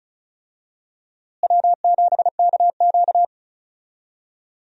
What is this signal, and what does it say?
Morse code, heard as a keyed tone, about 36 wpm, 700 Hz. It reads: W7KQ
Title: USA Call Signs - 4 Characters - Repeated 3x - 35wpm